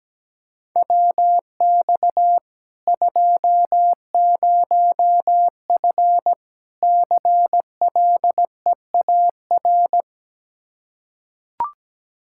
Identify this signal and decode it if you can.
Morse code, heard as a keyed tone, about 17 wpm, 700 Hz. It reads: WX 20F CLEAR
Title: K1USN SST QSO Practice 17wpm